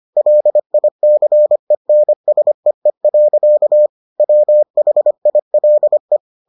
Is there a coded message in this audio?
Yes: Morse code from 0.2 to 6.2 s, heard as a keyed tone, about 25 wpm, 600 Hz. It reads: LICENSEE. WHILE